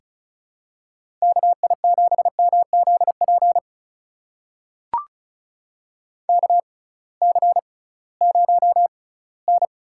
Morse code, heard as a keyed tone, about 34 wpm, spaced as for 20 wpm, 700 Hz. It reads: KI7MZP KC0N
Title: USA Call Signs - 6 Characters - 6x Character Spacing - 35wpm